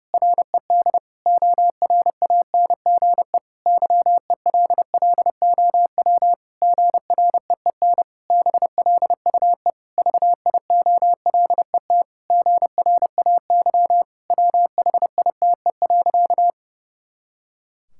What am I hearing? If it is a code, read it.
Morse code, heard as a keyed tone, about 32 wpm, 700 Hz. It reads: RED ORANGE YELLOW GREEN BLUE VIOLET GRAY WHITE.
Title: Pangram Challenge v2 - 30wpm